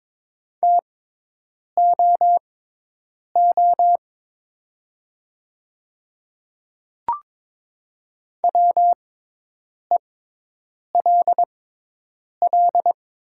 Morse code, heard as a keyed tone, about 22 wpm, spaced as for 8 wpm, 700 Hz. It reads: TOO WELL
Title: 6x Character Spacing - Top 500 Words - 22wpm